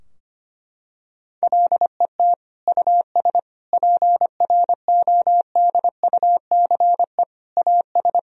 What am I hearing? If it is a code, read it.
Morse code, heard as a keyed tone, about 25 wpm, 700 Hz. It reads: LET US PRODUCE AS
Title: Sentences from Top 1000 Words - Review Part 3 - 25wpm